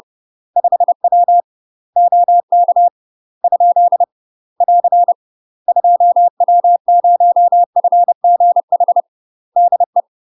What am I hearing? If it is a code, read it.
Morse code, heard as a keyed tone, about 30 wpm, 700 Hz. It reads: 5W OK ? <AR> 2W0FGH DE
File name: UK MASTER QSOs TRAINING / 30WPM / UK MASTER QSOs 0028